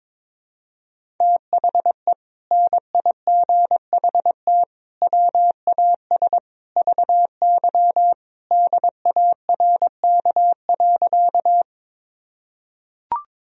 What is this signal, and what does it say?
Morse code, heard as a keyed tone, about 22 wpm, 700 Hz. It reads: THE NIGHT WAS VY DARK.
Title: Sentences from Top 500 Words and Abbreviations 22wpm